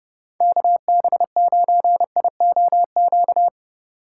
Morse code, heard as a keyed tone, about 30 wpm, 700 Hz. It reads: KB9IOQ